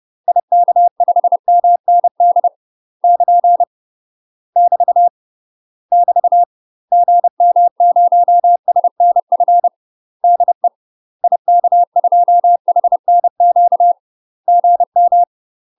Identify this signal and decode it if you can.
Morse code, heard as a keyed tone, about 30 wpm, 700 Hz. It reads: IK5MND <KN> <BT> <BT> GM0SNF DE IK2HNQ GM
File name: UK MASTER QSOs TRAINING / 30WPM / UK MASTER QSOs 0041